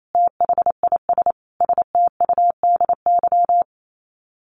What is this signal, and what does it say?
Morse code, heard as a keyed tone, about 28 wpm, 700 Hz. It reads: THIS STUDY
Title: Sets of 5 Words 28wpm